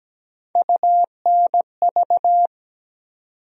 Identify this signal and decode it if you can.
Morse code, heard as a keyed tone, about 17 wpm, 700 Hz. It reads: UNV
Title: USA Call Sign Postfixes - 3 Letters 17wpm